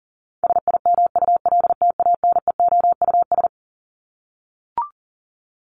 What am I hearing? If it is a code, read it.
Morse code, heard as a keyed tone, about 40 wpm, 700 Hz. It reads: SIMULTANEOUS
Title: LY Suffix - Top 100 Words with Base Words 40wpm